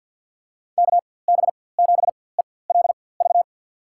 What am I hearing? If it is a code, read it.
Morse code, heard as a keyed tone, about 47 wpm, spaced as for 23 wpm, 700 Hz. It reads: KB7ELV
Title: USA Call Signs - 6 Characters - 4x Character Spacing - 50wpm